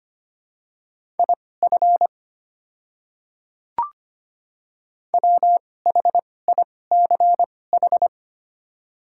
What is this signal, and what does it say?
Morse code, heard as a keyed tone, about 25 wpm, spaced as for 18 wpm, 690 Hz. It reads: IF WHICH
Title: 2x Character Spacing - Top 100 Words - 25wpm